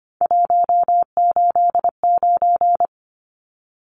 Morse code, heard as a keyed tone, about 25 wpm, 700 Hz. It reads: 189
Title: Numbers - 3-Digit 25wpm